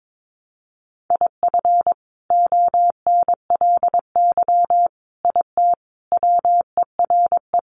Morse code, heard as a keyed tone, about 22 wpm, 700 Hz. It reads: IF ONLY IT WERE
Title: Sentences from Top 500 Words 22wpm